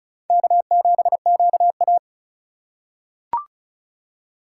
Morse code, heard as a keyed tone, about 36 wpm, 700 Hz. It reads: K7QA
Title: USA Call Signs - 4 Characters - 2x Character Spacing - 35wpm